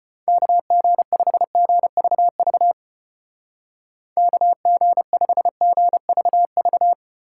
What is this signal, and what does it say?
Morse code, heard as a keyed tone, about 32 wpm, 700 Hz. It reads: KG5GVV KG5GVV